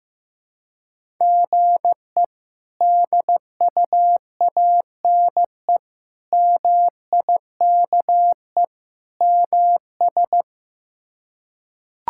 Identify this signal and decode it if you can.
Morse code, heard as a keyed tone, about 15 wpm, 700 Hz. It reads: GE DUANE MIKE MS
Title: K1USN SST QSO Practice - 15wpm